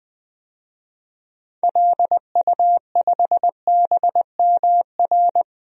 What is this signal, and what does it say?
Morse code, heard as a keyed tone, about 20 wpm, 700 Hz. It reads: LU5BMR
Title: DXCC-100 Call Signs - 6 Characters - 20wpm